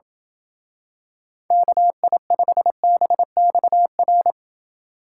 Morse code, heard as a keyed tone, about 28 wpm, 700 Hz. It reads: KI5BXR